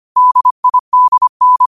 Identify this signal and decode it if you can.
Morse code, heard as a keyed tone, about 25 wpm, 1000 Hz. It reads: DIDN
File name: qotdfiles / 25wpm